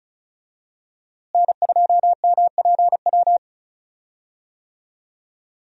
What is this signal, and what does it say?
Morse code, heard as a keyed tone, about 34 wpm, 700 Hz. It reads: N2MPW
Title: USA Call Signs - 5 Characters - Repeated 3x - 35wpm